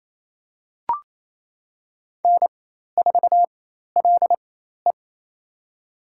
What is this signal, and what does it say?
Morse code, heard as a keyed tone, about 28 wpm, spaced as for 13 wpm, 700 Hz. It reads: N4LE
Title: USA Call Signs - 4 Characters - 4x Character Spacing - 28wpm